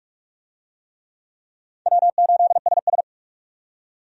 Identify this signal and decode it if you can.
Morse code, heard as a keyed tone, about 47 wpm, 700 Hz. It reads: W8SS
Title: USA Call Signs - 4 Characters - 45wpm